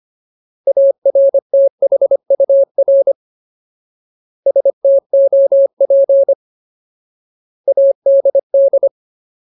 Morse code, heard as a keyed tone, about 25 wpm, 550 Hz. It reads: ARTHUR STOP ADD
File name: Ultimate Word List / 25WPM Ultimate